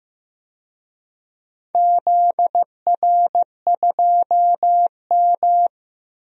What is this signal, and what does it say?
Morse code, heard as a keyed tone, about 15 wpm, 700 Hz. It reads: ZR2M
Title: DXCC-100 Call Signs - 4 Characters - 15wpm